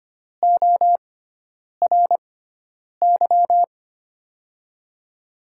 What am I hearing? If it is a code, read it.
Morse code, heard as a keyed tone, about 25 wpm, spaced as for 9 wpm, 700 Hz. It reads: ORY
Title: USA Call Signs - 6x Character Spacing - 25wpm